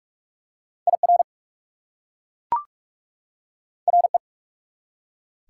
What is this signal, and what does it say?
Morse code, heard as a keyed tone, about 44 wpm, 690 Hz. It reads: IR RE
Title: Prefixes and Suffixes 45wpm